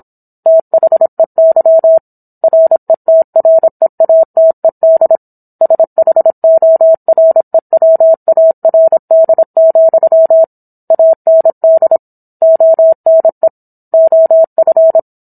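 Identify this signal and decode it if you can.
Morse code, heard as a keyed tone, about 27 wpm, 650 Hz. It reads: THEY RETREATED SHOREWARD, AND ONE OF
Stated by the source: Morse code practice of a book chapter at 26 WPM.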